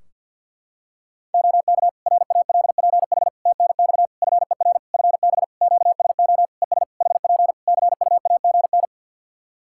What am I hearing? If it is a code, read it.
Morse code, heard as a keyed tone, about 46 wpm, 700 Hz. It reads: OK RALPH TNX FER FB QSO ES HP CUAGN
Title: Practice QSO - Exchange 4 - Ending - 50wpm